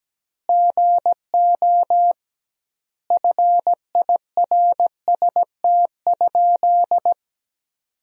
Morse code, heard as a keyed tone, about 17 wpm, 700 Hz. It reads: GO FIRST?
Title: 2x Word-Spacing - Sentences from Top 100 Words - 17wpm